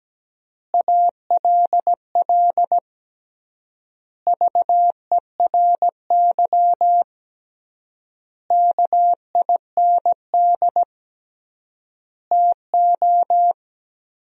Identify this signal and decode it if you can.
Morse code, heard as a keyed tone, about 17 wpm, 700 Hz. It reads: ALL VERY KIND TO
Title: 3x Word-Spacing - Sentences from Top 200 Words - 17wpm